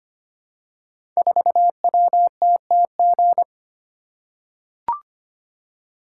Morse code, heard as a keyed tone, about 25 wpm, 700 Hz. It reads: 4WTTG
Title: Code Groups - 5 Letter:Numbers - 25wpm